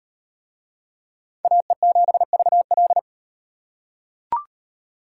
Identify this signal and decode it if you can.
Morse code, heard as a keyed tone, about 35 wpm, 700 Hz. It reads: AE7VL